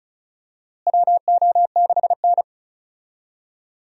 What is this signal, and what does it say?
Morse code, heard as a keyed tone, about 36 wpm, 700 Hz. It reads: WO6N
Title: USA Call Signs - 4 Characters - Repeated 2x - 35wpm